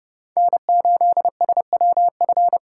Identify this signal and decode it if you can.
Morse code, heard as a keyed tone, about 32 wpm, 700 Hz. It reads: N8SWF
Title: USA Call Signs - 5 Characters - 3x Character Spacing - 30wpm